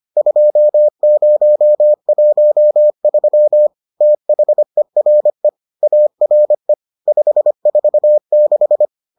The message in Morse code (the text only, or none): 2013 THERE ARE 546